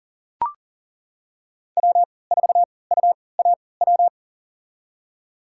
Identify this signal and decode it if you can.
Morse code, heard as a keyed tone, about 40 wpm, spaced as for 23 wpm, 700 Hz. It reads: W4UAW
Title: USA Call Signs - 5 Characters - 3x Character Spacing - 40wpm